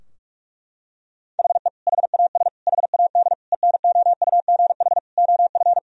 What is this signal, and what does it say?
Morse code, heard as a keyed tone, about 46 wpm, 700 Hz. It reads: HE HAS HAD ENOUGH OF
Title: Sentences from Top 1000 Words - Review Part 3 - 45wpm